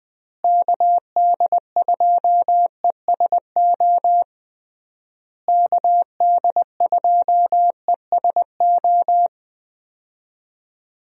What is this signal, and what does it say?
Morse code, heard as a keyed tone, about 20 wpm, 700 Hz. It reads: KD2ESO KD2ESO
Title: USA Call Signs - 6 Characters - Repeated 3x - 20wpm